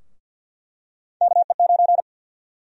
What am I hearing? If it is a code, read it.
Morse code, heard as a keyed tone, about 48 wpm, 700 Hz. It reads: KE9
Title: USA Call Sign Prefixes - Repeated 2x - 50wpm